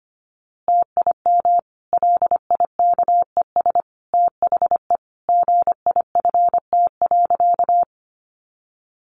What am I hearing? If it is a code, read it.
Morse code, heard as a keyed tone, about 25 wpm, 700 Hz. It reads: TIM LIKES THE GIFT.
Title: Easy Phrases 25wpm